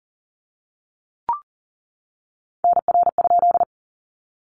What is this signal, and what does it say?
Morse code, heard as a keyed tone, about 40 wpm, 700 Hz. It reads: NR?
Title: Common QSO Phrases 40wpm